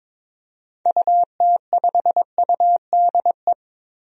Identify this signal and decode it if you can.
Morse code, heard as a keyed tone, about 22 wpm, 700 Hz. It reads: UT5UDE
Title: DXCC-100 Call Signs - 6 Characters - 22wpm